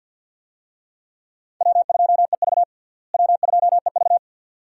Morse code, heard as a keyed tone, about 46 wpm, 700 Hz. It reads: W2EV W2EV
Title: USA Call Signs - 4 Characters - Repeated 2x - 50wpm